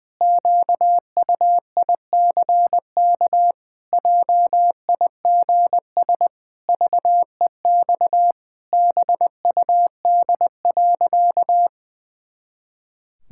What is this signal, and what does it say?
Morse code, heard as a keyed tone, about 21 wpm, 700 Hz. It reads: QUICK JIGS VEX BUD.
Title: Pangram Challenge v2 - 20wpm